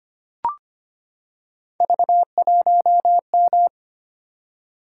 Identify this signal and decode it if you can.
Morse code, heard as a keyed tone, about 25 wpm, 700 Hz. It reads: V1M
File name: Three Letter-Number Combinations - 25wpm